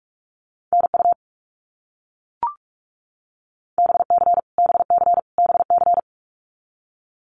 Morse code, heard as a keyed tone, about 44 wpm, 700 Hz. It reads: NU BC BC BC
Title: Canadian Provinces and Territories - Repeated 3x - 45wpm